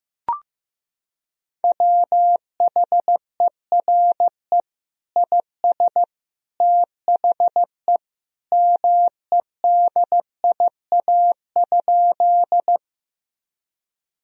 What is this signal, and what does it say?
Morse code, heard as a keyed tone, about 15 wpm, 700 Hz. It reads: WHERE IS THE MEDIA?
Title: Sentences from Top 1000 Words - Review Part 4 - 15wpm